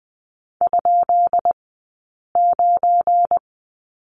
Morse code, heard as a keyed tone, about 20 wpm, 700 Hz. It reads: ? 9
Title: 190 Lesson 31d - Letter v - Two Characters 20wpm